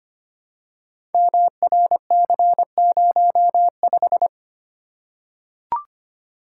Morse code, heard as a keyed tone, about 25 wpm, 700 Hz. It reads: MRC05
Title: Code Groups - 5 Letter:Numbers - 25wpm